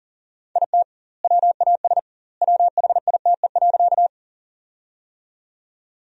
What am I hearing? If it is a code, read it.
Morse code, heard as a keyed tone, about 40 wpm, 700 Hz. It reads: IT WAS WHITE.
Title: 2x Word-Spacing - Sentences from Top 300 Words - 40wpm